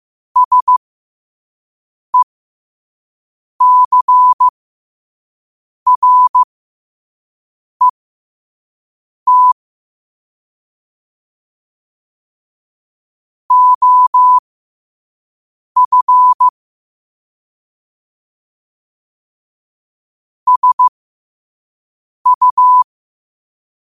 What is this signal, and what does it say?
Morse code, heard as a keyed tone, about 15 wpm, spaced as for 5 wpm, 1000 Hz. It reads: SECRET OF SU
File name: qotdfiles / 5wpm